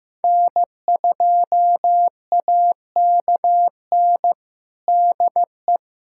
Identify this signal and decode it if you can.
Morse code, heard as a keyed tone, about 16 wpm, 700 Hz. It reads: N2AKN DE
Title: USA Callsign DE Callsign - 5 Characters - 15wpm